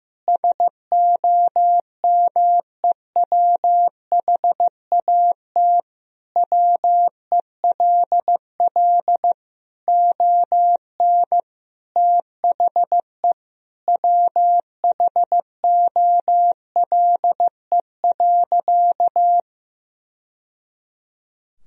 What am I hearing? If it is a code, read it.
Morse code, heard as a keyed tone, about 15 wpm, 700 Hz. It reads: SOMEWHAT WELL ON THE WHOLE.
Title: Sentences from Top 1000 Words - Review Part 1 - 15wpm